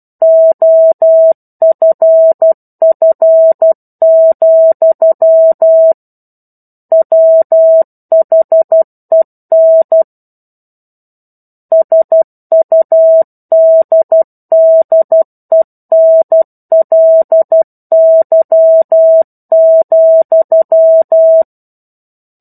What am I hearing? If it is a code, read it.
Morse code, heard as a keyed tone, about 12 wpm, 650 Hz. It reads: OFF, WHEN SUDDENLY,
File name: 01-Chapter-01 12 WPM